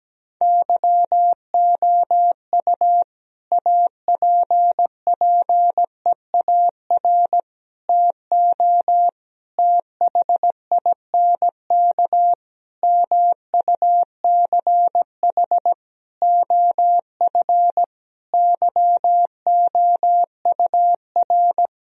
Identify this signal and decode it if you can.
Morse code, heard as a keyed tone, about 17 wpm, 700 Hz. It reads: YOU APPEAR TO THINK MUCH OF YOUR
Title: Sentences from Top 1000 Words - Review Part 2 - 17wpm